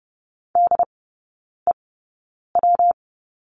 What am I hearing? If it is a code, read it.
Morse code, heard as a keyed tone, about 30 wpm, spaced as for 9 wpm, 700 Hz. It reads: DEW
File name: Lesson 19e - Letter w - Three Characters 30wpm